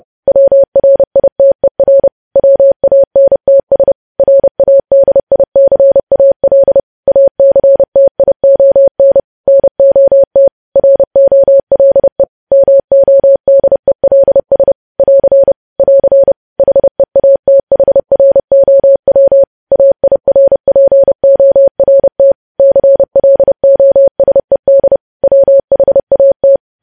Morse code, heard as a keyed tone, about 30 wpm, 550 Hz. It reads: WRITER WANTS RADICAL ACTION NOT ROLE MODELS <AR> <AR> HEATHROW AIRPORT CLOSED WHAT
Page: news headlines in morse code at 30 wpm